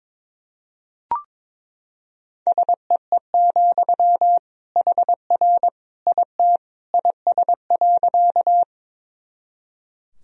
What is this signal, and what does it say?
Morse code, heard as a keyed tone, about 22 wpm, 700 Hz. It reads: SEE, HR IT IS.
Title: Sentences from Top 200 Words and Abbreviations 22wpm